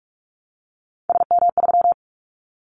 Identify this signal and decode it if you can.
Morse code, heard as a keyed tone, about 44 wpm, 700 Hz. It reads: SM3
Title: Callsign Prefixes for Top 25 DXCC - Repeated 3x - 45wpm